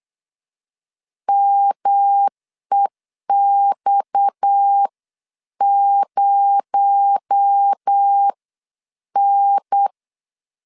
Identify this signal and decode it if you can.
Morse code, heard as a keyed tone, about 8 wpm, spaced as for 7 wpm, 800 Hz. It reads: MEX0N
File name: CW / DET / 1 6 1500C S B 8wpm 5142